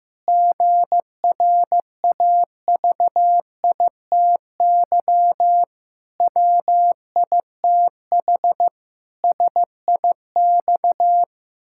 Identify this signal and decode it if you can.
Morse code, heard as a keyed tone, about 15 wpm, 700 Hz. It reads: GRAVITY WITH SIX
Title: Pangram Challenge v2 - 15wpm